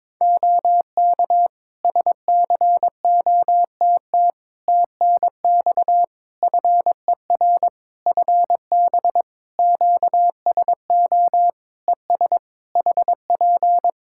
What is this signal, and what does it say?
Morse code, heard as a keyed tone, about 22 wpm, 700 Hz. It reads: OK SCOTT TNX FER FB QSO ES HP
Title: Practice QSO - Exchange 4 - Ending - 22wpm